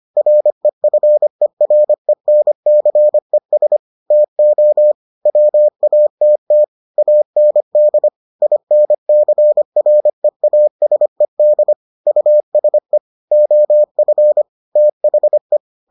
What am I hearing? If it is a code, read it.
Morse code, heard as a keyed tone, about 25 wpm, 600 Hz. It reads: REFERENCES TO WATT AND INCREASED USE OF THE